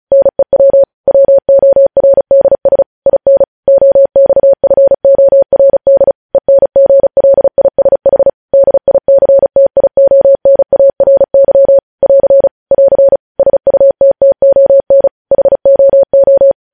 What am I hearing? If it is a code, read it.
Morse code, heard as a keyed tone, about 36 wpm, 550 Hz. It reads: NEW WORDS IN OXFORD ENGLISH DICTIONARY <AR> <AR> SUTTON HOO
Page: News headlines in Morse code at 35 wpm